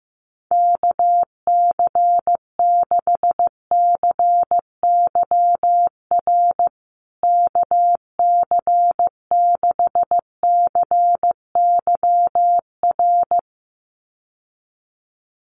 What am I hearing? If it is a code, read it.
Morse code, heard as a keyed tone, about 15 wpm, 700 Hz. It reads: KC6CYR KC6CYR
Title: USA Call Signs Repeated 2x 15wpm